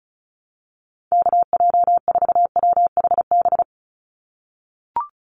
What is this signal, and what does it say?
Morse code, heard as a keyed tone, about 34 wpm, 700 Hz. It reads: KJ4WHB
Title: USA Call Signs 35wpm